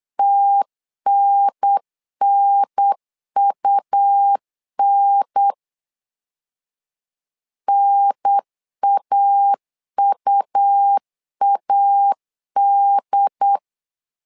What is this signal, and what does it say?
Morse code, heard as a keyed tone, about 8 wpm, 800 Hz. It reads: TNNUN NAUAD
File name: LEZIONI / 3 MWRO / 3 M 8wpm 2300